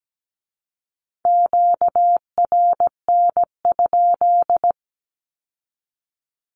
Morse code, heard as a keyed tone, about 17 wpm, 700 Hz. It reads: QRN?
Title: Q-Codes with Spoken Meaning - 17wpm